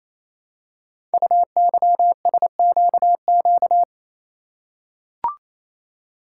Morse code, heard as a keyed tone, about 27 wpm, 700 Hz. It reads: UYSQQ
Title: Code Groups - 5 Letters - 28wpm